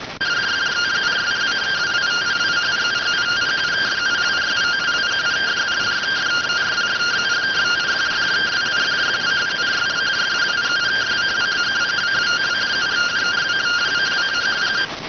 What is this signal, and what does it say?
Two-tone teleprinter signal, about 170 Hz shift.